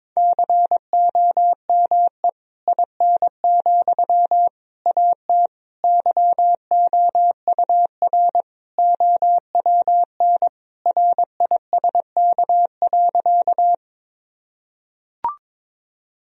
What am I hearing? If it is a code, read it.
Morse code, heard as a keyed tone, about 22 wpm, 700 Hz. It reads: COME IN, AT YOUR OWN RISK.
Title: Sentences from Top 1000 Words - Review Part 4 - 22wpm